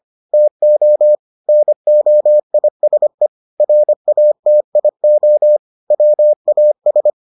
Morse code, heard as a keyed tone, about 25 wpm, 600 Hz. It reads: TO NOISE RATIO WAS